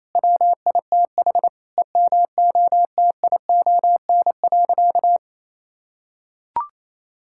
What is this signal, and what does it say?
Morse code, heard as a keyed tone, about 27 wpm, 700 Hz. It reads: WITH EMOTION.
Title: Sentences from Top 1600 Words 28wpm